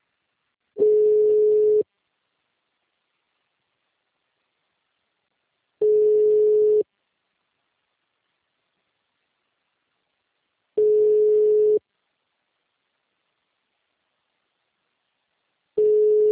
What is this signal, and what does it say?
Ringback tone (Europe, 425 Hz).
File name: out- -2012- - - .4033